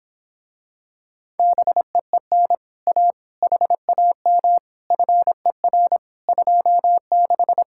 Morse code, heard as a keyed tone, about 26 wpm, 700 Hz. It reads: BEEN A HAM FER 26